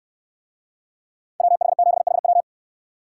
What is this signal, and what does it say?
Morse code, heard as a keyed tone, about 69 wpm, 700 Hz. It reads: WH7HW